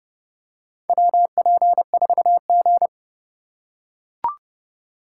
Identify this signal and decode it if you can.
Morse code, heard as a keyed tone, about 30 wpm, 700 Hz. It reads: WP4G
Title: USA Call Signs - 4 Characters - 4x Character Spacing - 30wpm